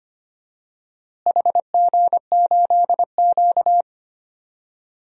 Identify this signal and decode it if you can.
Morse code, heard as a keyed tone, about 25 wpm, 700 Hz. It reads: HG8Q
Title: DXCC-100 Call Signs - 4 Characters - Repeated 3x - 25wpm